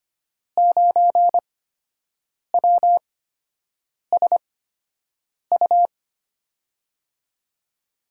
Morse code, heard as a keyed tone, about 25 wpm, spaced as for 7 wpm, 700 Hz. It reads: 9WSU
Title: USA Call Signs - 8x Character Spacing - 25wpm